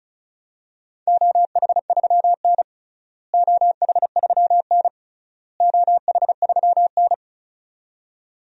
Morse code, heard as a keyed tone, about 34 wpm, 700 Hz. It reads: OH3N OH3N OH3N
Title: DXCC-100 Call Signs - 4 Characters - Repeated 3x - 35wpm